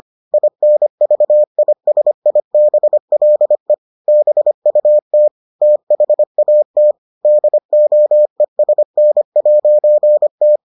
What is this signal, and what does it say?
Morse code, heard as a keyed tone, about 25 wpm, 600 Hz. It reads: INVISIBLE BUT THAT DOESN'T